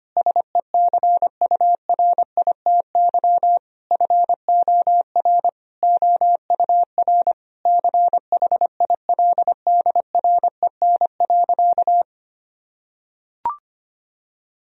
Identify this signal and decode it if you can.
Morse code, heard as a keyed tone, about 25 wpm, 700 Hz. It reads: SECURITY FOR OUR CHILDREN.
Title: Sentences from Top 1000 Words - Review Part 4 - 25wpm